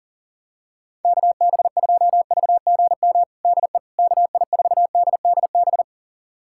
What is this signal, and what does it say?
Morse code, heard as a keyed tone, about 40 wpm, 700 Hz. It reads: KB2VGM DE KI4DDB
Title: USA Callsign DE Callsign - 6 Characters - 40wpm